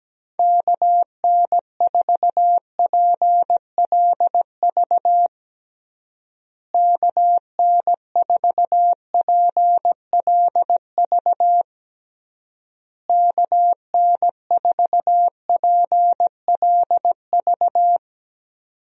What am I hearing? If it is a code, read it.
Morse code, heard as a keyed tone, about 17 wpm, 700 Hz. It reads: KN4PLV KN4PLV KN4PLV
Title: USA Call Signs - 6 Characters - Repeated 3x - 17wpm